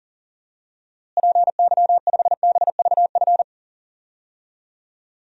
Morse code, heard as a keyed tone, about 40 wpm, 700 Hz. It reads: PY5BVF
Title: DXCC-100 Call Signs - 6 Characters - Repeated 3x - 40wpm